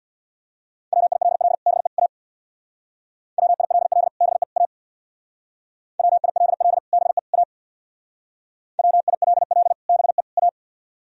Morse code, heard as a keyed tone, about 61 wpm, 700 Hz. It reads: WILL BE A WILL BE A WILL BE A WILL BE A
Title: Speed-Racing - Sets of 3 Words 50wpm